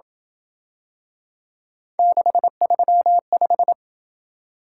Morse code, heard as a keyed tone, about 26 wpm, 700 Hz. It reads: 635